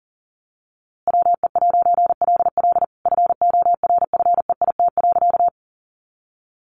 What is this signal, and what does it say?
Morse code, heard as a keyed tone, about 40 wpm, 700 Hz. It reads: WE'LL FORFEIT.
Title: A Sound Of Thunder 40wpm